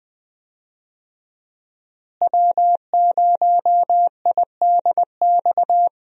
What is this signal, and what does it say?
Morse code, heard as a keyed tone, about 20 wpm, 700 Hz. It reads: W0IDX
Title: USA Call Signs - 5 Characters - Repeated 3x - 20wpm